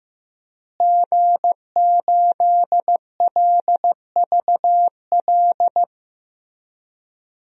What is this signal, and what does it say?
Morse code, heard as a keyed tone, about 15 wpm, 700 Hz. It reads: G8LVL
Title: DXCC-100 Call Signs - 5 Characters - Repeated 3x - 15wpm